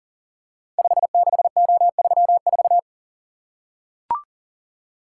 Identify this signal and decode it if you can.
Morse code, heard as a keyed tone, about 39 wpm, 700 Hz. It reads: 56O34
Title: Code Groups - 5 Letter:Numbers - 40wpm